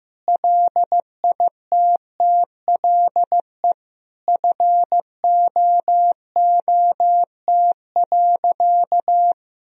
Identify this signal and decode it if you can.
Morse code, heard as a keyed tone, about 15 wpm, 700 Hz. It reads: LITTLE FOOT.
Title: Sentences from Top 1000 Words - Review Part 3 - 15wpm